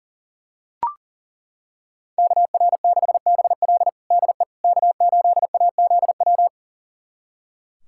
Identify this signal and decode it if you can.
Morse code, heard as a keyed tone, about 40 wpm, 700 Hz. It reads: KR6BL DE K8AZW
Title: USA Callsign DE Callsign - 5 Characters - 40wpm